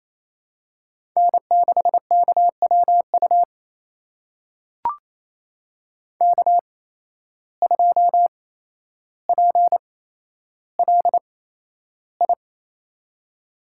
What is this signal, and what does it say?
Morse code, heard as a keyed tone, about 28 wpm, spaced as for 12 wpm, 700 Hz. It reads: N6KWU K2PLI
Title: USA Call Signs - 5 Characters - 8x Character Spacing - 28wpm